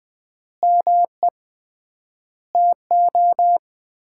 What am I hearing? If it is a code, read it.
Morse code, heard as a keyed tone, about 20 wpm, 700 Hz. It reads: ME TO
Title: 3x Word-Spacing - Sets of 4 Words - 20wpm